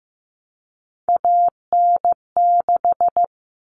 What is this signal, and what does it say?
Morse code, heard as a keyed tone, about 15 wpm, 700 Hz. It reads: AN6
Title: Common Cut Numbers - 3-Digit 15wpm